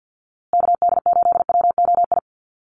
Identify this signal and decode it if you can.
Morse code, heard as a keyed tone, about 48 wpm, 700 Hz. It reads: KD8WOI
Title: USA Call Signs Repeated 2x 50wpm